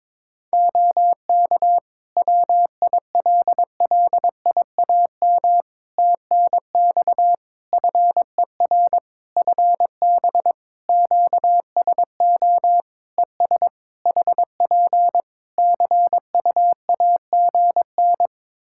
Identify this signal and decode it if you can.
Morse code, heard as a keyed tone, about 22 wpm, 700 Hz. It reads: OK WILLIAM TNX FER FB QSO ES HP CUAGN
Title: Practice QSO - Exchange 4 - Ending - 22wpm